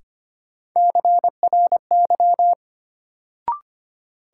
Morse code, heard as a keyed tone, about 25 wpm, 700 Hz. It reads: CRY
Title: Sets of 5 Words Encore - New Word Drill - 25wpm